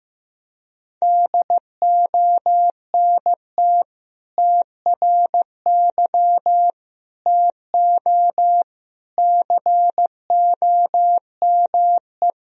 Morse code, heard as a keyed tone, about 15 wpm, 700 Hz. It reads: DONT TRY TO COME
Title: Sentences from Top 1000 Words - Review Part 1 - 15wpm